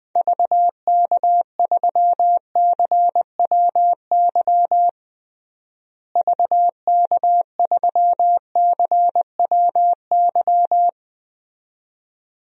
Morse code, heard as a keyed tone, about 20 wpm, 700 Hz. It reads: VK3CWY VK3CWY
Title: DXCC-100 Call Signs - 6 Characters - Repeated 2x - 20wpm